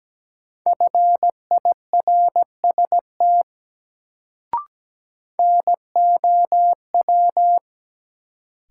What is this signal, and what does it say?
Morse code, heard as a keyed tone, about 17 wpm, 700 Hz. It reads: FIRST NOW
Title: Sentences from Top 100 Words - New Word Drill - 17wpm